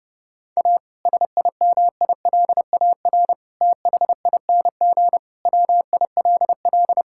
Morse code, heard as a keyed tone, about 30 wpm, 700 Hz. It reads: A SIMILAR THING WILL
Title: Sentences from Top 1000 Words - Review Part 5 - 30wpm